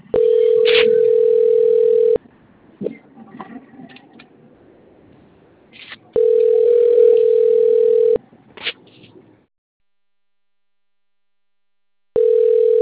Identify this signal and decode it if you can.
Ringback tone (North America), 440 + 480 Hz.
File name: rg-1600-unknown- - - .8403